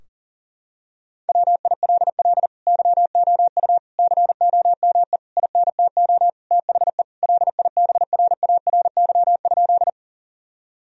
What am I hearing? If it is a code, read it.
Morse code, heard as a keyed tone, about 40 wpm, 700 Hz. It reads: WILL YOU COME INTO THE LIBRARY?
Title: Sentences from Top 1000 Words - Review Part 4 - 40wpm